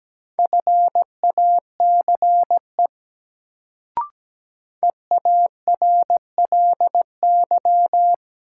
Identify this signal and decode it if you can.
Morse code, heard as a keyed tone, about 17 wpm, 700 Hz. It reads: FACE EARLY
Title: Sets of 3 Words - New Word Drill - 17wpm